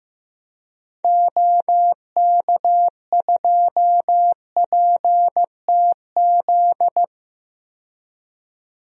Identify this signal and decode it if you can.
Morse code, heard as a keyed tone, about 15 wpm, 700 Hz. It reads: OK2PTZ
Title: DXCC-100 Call Signs - 6 Characters - 15wpm